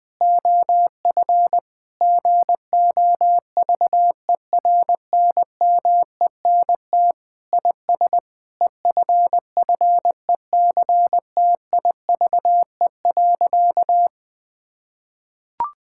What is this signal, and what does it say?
Morse code, heard as a keyed tone, about 21 wpm, 700 Hz. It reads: OF GOVERNMENT IS EFFECTIVE.
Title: Sentences from Top 1000 Words - Review Part 6 - 20wpm